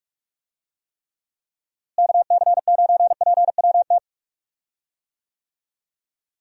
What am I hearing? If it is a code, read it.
Morse code, heard as a keyed tone, about 43 wpm, 700 Hz. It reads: KC9PWT
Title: USA Call Signs - 6 Characters - Repeated 2x - 45wpm